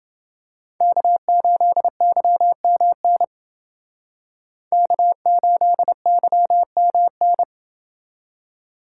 Morse code, heard as a keyed tone, about 28 wpm, 700 Hz. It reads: K8YMN K8YMN